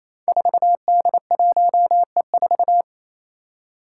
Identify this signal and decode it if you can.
Morse code, heard as a keyed tone, about 28 wpm, 700 Hz. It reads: 4D1E4
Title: Five Challenging Letter-Number Combinations 28wpm